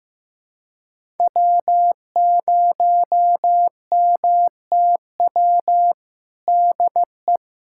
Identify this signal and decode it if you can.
Morse code, heard as a keyed tone, about 15 wpm, 700 Hz. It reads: W0MTW DE
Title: USA Callsign DE Callsign - 5 Characters - 15wpm